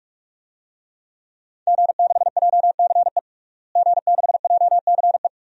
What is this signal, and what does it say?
Morse code, heard as a keyed tone, about 41 wpm, 700 Hz. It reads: G6JCE G6JCE
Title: DXCC-100 Call Signs - 5 Characters - Repeated 2x - 45wpm